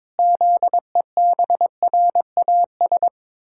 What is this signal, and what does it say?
Morse code, heard as a keyed tone, about 22 wpm, 700 Hz. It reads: ZEBRAS
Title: Pangram Challenge v2 - 22wpm